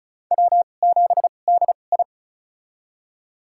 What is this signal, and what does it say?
Morse code, heard as a keyed tone, about 36 wpm, spaced as for 25 wpm, 700 Hz. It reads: W7DI
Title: USA Call Signs - 2x Character Spacing - 35wpm